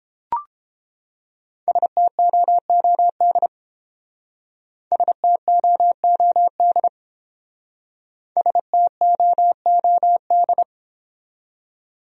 Morse code, heard as a keyed tone, about 28 wpm, 700 Hz. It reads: STOOD STOOD STOOD